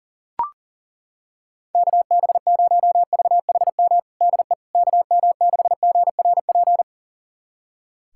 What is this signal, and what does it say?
Morse code, heard as a keyed tone, about 40 wpm, 700 Hz. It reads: KB0VHM DE KM6GRP
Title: USA Callsign DE Callsign - 6 Characters - 40wpm